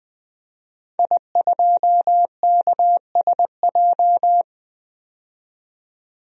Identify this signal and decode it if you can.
Morse code, heard as a keyed tone, about 20 wpm, 700 Hz. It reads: I2KSJ
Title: DXCC-100 Call Signs - 5 Characters - 20wpm